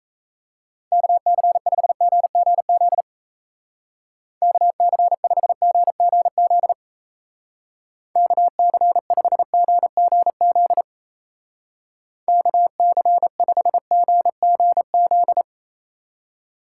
Morse code, heard as a keyed tone, about 34 wpm, 700 Hz. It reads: KC5GGZ KC5GGZ KC5GGZ KC5GGZ
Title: Speed Racing - USA 2x3 Call Signs 28wpm